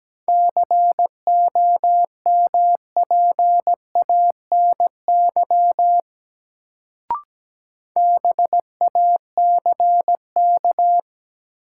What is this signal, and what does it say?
Morse code, heard as a keyed tone, about 17 wpm, 700 Hz. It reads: COMPANY BACK
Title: Sets of 2 Words - New Word Drill - 17wpm